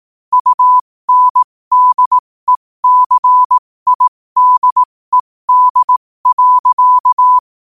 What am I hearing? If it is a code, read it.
Morse code, heard as a keyed tone, about 18 wpm, spaced as for 15 wpm, 1000 Hz. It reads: UNDECIDED.
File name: qotdfiles / 15wpm